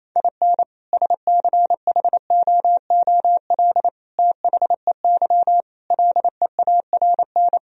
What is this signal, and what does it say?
Morse code, heard as a keyed tone, about 28 wpm, 700 Hz. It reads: IN SCHOOL THEY LEARN
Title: Sentences from Top 1000 Words - Review Part 4 - 28wpm